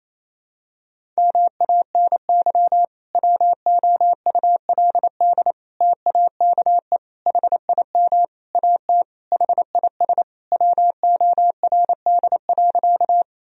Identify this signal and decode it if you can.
Morse code, heard as a keyed tone, about 28 wpm, 700 Hz. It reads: MANY WOULD TAKE HIM AT HIS WORD.
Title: Sentences from Top 1000 Words - Review Part 1 - 28wpm